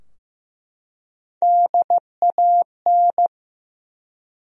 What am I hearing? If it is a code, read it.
Morse code, heard as a keyed tone, about 15 wpm, 700 Hz. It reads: DAN
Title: USA Common Names - Max 5 Letters - 15wpm